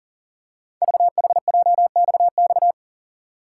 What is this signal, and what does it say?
Morse code, heard as a keyed tone, about 41 wpm, 700 Hz. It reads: VHJXX
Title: Code Groups - 5 Letters - 40wpm